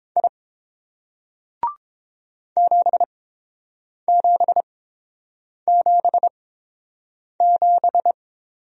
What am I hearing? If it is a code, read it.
Morse code, heard as a keyed tone, about 30 wpm, spaced as for 7 wpm, 700 Hz. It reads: I 7777